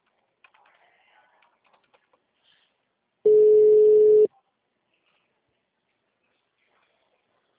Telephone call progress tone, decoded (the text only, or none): ringback tone